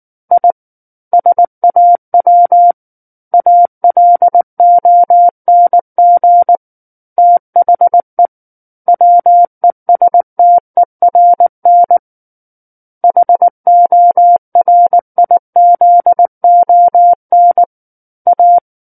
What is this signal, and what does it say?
Morse code, heard as a keyed tone, about 19 wpm, 700 Hz. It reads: I SAW ALONG THE WESTERN HORIZON A
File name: Book One / 10-Chapter 10 - 19 WPM